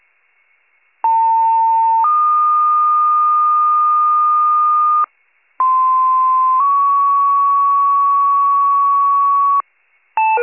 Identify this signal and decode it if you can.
Two-tone page, heard as a one-second tone, then a three-second tone, about 900 then 1220 Hz, about 1040 then 1120 Hz. It is heard from 1.0 to 9.6 s.